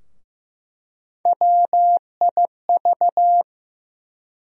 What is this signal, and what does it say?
Morse code, heard as a keyed tone, about 15 wpm, 700 Hz. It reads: WIV
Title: USA Call Sign Postfixes - 3 Letters 15wpm